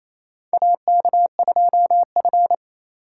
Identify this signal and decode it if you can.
Morse code, heard as a keyed tone, about 28 wpm, 700 Hz. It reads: AK2F
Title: USA Call Signs - 3x Character Spacing - 28wpm